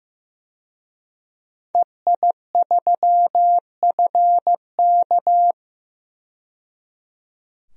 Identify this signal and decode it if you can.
Morse code, heard as a keyed tone, about 15 wpm, 700 Hz. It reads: EI3FK
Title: DXCC-100 Call Signs - 5 Characters - 15wpm